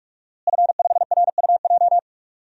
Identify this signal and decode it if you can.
Morse code, heard as a keyed tone, about 41 wpm, 700 Hz. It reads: F5RUJ
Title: DXCC-100 Call Signs - 5 Characters - Repeated 3x - 45wpm